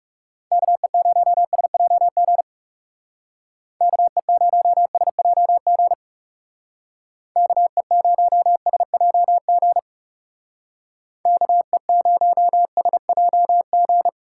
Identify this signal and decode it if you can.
Morse code, heard as a keyed tone, about 36 wpm, 700 Hz. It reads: KE0SJG KE0SJG KE0SJG KE0SJG
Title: Speed Racing - USA Call Signs 30wpm